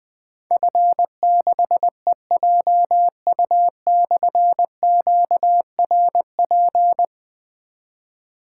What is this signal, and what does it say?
Morse code, heard as a keyed tone, about 20 wpm, 700 Hz. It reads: F6EJU/QRP
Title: Difficult Call Signs 20wpm